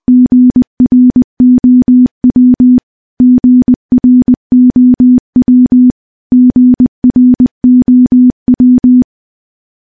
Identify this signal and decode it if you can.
Morse code, heard as a keyed tone, about 20 wpm, 250 Hz. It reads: GROW GROW GROW
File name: morse / morse-training-20WPM